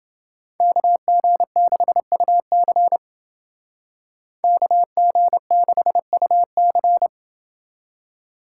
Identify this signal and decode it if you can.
Morse code, heard as a keyed tone, about 28 wpm, 700 Hz. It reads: KG6UC KG6UC